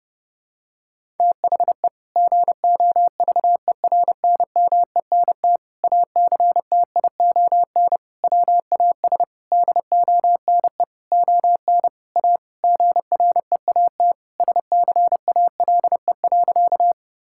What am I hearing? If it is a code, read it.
Morse code, heard as a keyed tone, about 30 wpm, 700 Hz. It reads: THE GOVERNMENT ACTION WAS DONE ON A GREAT SCALE.
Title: Sentences from Top 1000 Words - Review Part 5 - 30wpm